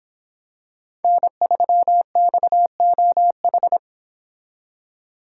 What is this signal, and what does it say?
Morse code, heard as a keyed tone, about 26 wpm, 700 Hz. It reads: N3XOH